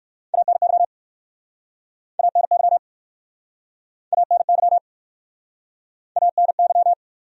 Morse code, heard as a keyed tone, about 68 wpm, 700 Hz. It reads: ANY ANY ANY ANY